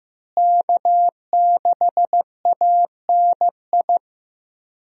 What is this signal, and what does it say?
Morse code, heard as a keyed tone, about 15 wpm, 700 Hz. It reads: K6ANI
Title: USA Call Signs - 5 Characters - Repeated 3x - 15wpm